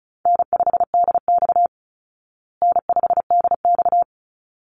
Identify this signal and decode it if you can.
Morse code, heard as a keyed tone, about 34 wpm, 700 Hz. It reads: N5DX N5DX
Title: Call Signs Repeated 3x - 35wpm